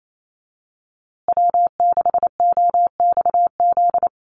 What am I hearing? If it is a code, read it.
Morse code, heard as a keyed tone, about 28 wpm, 700 Hz. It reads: W6OXZ
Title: USA Call Signs Repeated 2x 28wpm